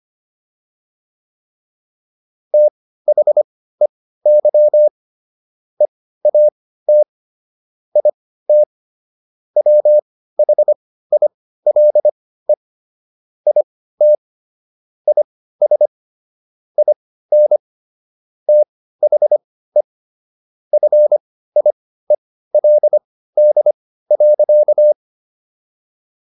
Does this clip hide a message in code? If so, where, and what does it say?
Yes: Morse code from 2.5 to 24.9 s, heard as a keyed tone, about 25 wpm, spaced as for 15 wpm, 600 Hz. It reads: THEY EAT IT WHILE IT IS IN THE FIELD.